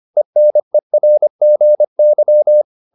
Morse code, heard as a keyed tone, about 25 wpm, 600 Hz. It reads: ENERGY